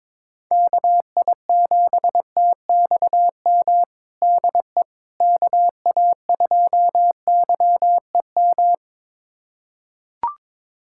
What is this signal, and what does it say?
Morse code, heard as a keyed tone, about 22 wpm, 700 Hz. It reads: KI7TXM DE KA2YEM
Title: USA Callsign DE Callsign - 6 Characters - 22wpm